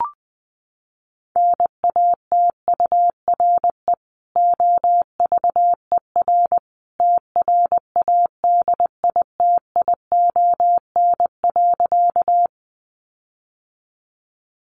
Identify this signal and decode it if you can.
Morse code, heard as a keyed tone, about 20 wpm, 700 Hz. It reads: NATURE OVER TRADITION.
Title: Sentences from Top 1500 Words 20wpm